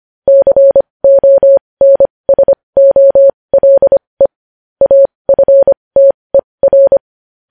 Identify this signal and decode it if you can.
Morse code, heard as a keyed tone, about 25 wpm, spaced as for 20 wpm, 550 Hz. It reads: CONSOLE AFTER
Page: news-headlines-in-morse-code-at-20-wpm